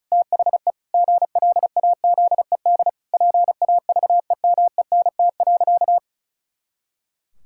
Morse code, heard as a keyed tone, about 36 wpm, 700 Hz. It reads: THE GLAZED PAVEMENT.
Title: Pangram Challenge v2 - 35wpm